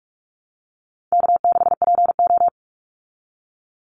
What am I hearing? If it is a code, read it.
Morse code, heard as a keyed tone, about 44 wpm, 700 Hz. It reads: K6PO
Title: Call Signs Repeated 2x - 45wpm